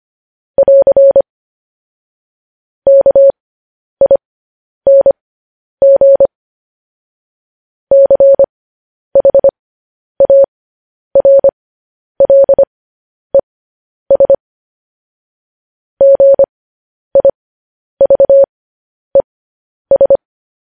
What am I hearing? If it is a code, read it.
Morse code, heard as a keyed tone, about 25 wpm, spaced as for 10 wpm, 550 Hz. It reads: <AR> KING CHARLES GIVES
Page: news headlines in morse code at 10 wpm